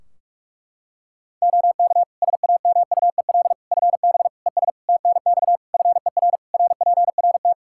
Morse code, heard as a keyed tone, about 46 wpm, 700 Hz. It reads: OK SAMUEL FB ES TNX FER RPRT
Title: K1USN SST QSO Practice 45wpm